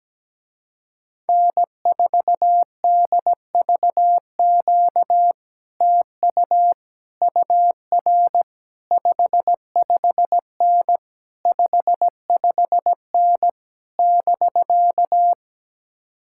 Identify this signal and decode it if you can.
Morse code, heard as a keyed tone, about 17 wpm, 700 Hz. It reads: N4DVQ TU UR 55N 55N <BK>
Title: Speed-Racing Common QSO Phrases - 17wpm